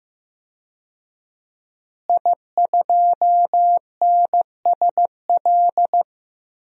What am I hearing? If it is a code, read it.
Morse code, heard as a keyed tone, about 16 wpm, 700 Hz. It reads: I2NSL
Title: DXCC-100 Call Signs - 5 Characters - Repeated 2x - 15wpm